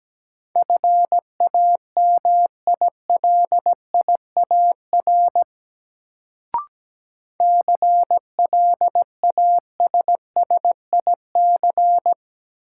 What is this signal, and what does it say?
Morse code, heard as a keyed tone, about 17 wpm, 700 Hz. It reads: FAMILIAR CLASSIC
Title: Sentences from Top 1000 Words - New Word Drill - 17wpm